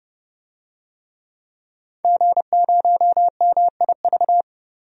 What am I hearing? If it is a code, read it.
Morse code, heard as a keyed tone, about 30 wpm, 700 Hz. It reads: G0MIV
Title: DXCC-100 Call Signs - 5 Characters - 30wpm